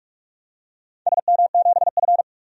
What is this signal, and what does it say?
Morse code, heard as a keyed tone, about 46 wpm, 700 Hz. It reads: SM7F
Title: DXCC-100 Call Signs - 4 Characters - 45wpm